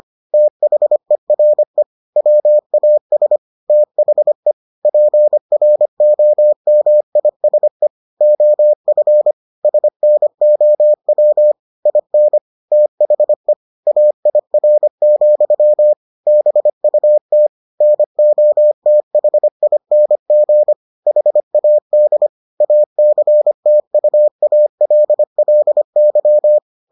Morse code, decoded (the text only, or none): THERE WAS THE PROMISE OF SNOW IN THE AIR, BUT NOTHING HAD ACTUALLY